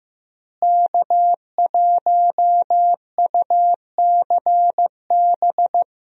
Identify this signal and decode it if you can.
Morse code, heard as a keyed tone, about 15 wpm, 700 Hz. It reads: K1UCB
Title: USA Call Signs - 5 Characters - Repeated 2x - 15wpm